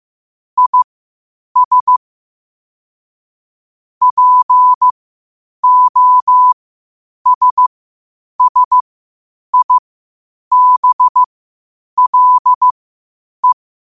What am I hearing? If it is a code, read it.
Morse code, heard as a keyed tone, about 15 wpm, spaced as for 8 wpm, 1000 Hz. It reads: IS POSSIBLE
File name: qotdfiles / 8wpm